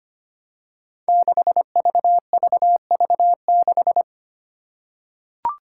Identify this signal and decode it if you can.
Morse code, heard as a keyed tone, about 25 wpm, 700 Hz. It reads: 6VVV6
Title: Five Challenging Letter-Number Combinations 25wpm